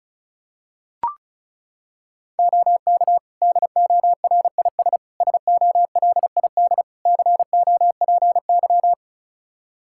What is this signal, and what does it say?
Morse code, heard as a keyed tone, about 36 wpm, 700 Hz. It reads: OK DORIS SOLID COPY
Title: Practice QSO - Exchange 3 - More Info - 35wpm